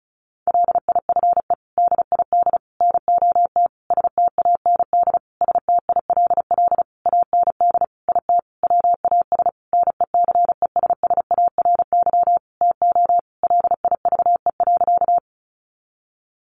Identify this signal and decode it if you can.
Morse code, heard as a keyed tone, about 34 wpm, 700 Hz. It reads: LIFE DID NOT STAND STILL AND IT WAS NECESSARY TO LIVE.
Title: Sentences from Top 800 Words 35wpm